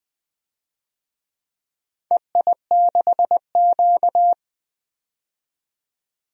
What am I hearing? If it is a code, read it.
Morse code, heard as a keyed tone, about 20 wpm, 700 Hz. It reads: EI6Q
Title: DXCC-100 Call Signs - 4 Characters - 20wpm